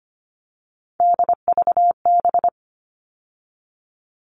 Morse code, heard as a keyed tone, about 25 wpm, 700 Hz. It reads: DVB
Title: All Cut Numbers - 3-Digit 25wpm